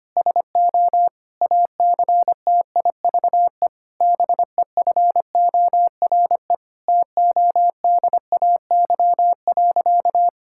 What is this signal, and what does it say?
Morse code, heard as a keyed tone, about 25 wpm, 700 Hz. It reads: SO ACTIVE BEFORE TODAY.
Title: Sentences from Top 1000 Words - Review Part 6 - 25wpm